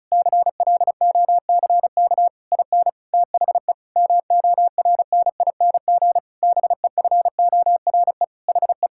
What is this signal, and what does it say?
Morse code, heard as a keyed tone, about 36 wpm, 700 Hz. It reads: CLOCK IN THE MORNING BEFORE HE
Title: Pangram Challenge v2 - 35wpm